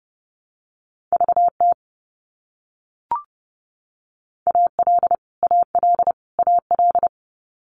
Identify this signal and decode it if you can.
Morse code, heard as a keyed tone, about 30 wpm, 700 Hz. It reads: VT AL AL AL
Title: US States Repeated 3x 30wpm